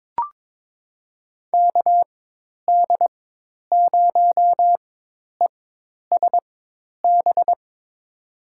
Morse code, heard as a keyed tone, about 21 wpm, spaced as for 10 wpm, 700 Hz. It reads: KD0ESB
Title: USA Call Signs - 6 Characters - 4x Character Spacing - 22wpm